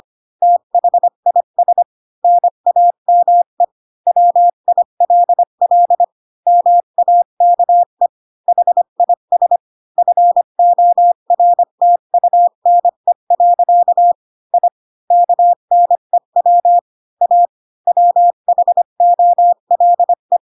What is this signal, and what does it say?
Morse code, heard as a keyed tone, about 25 wpm, 700 Hz. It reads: THIS NAME WILL MAKE HIS FORTUNE. I KNEW A WHOLE
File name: Pinocchio / Pinocchio ch3 20 25 ik1hge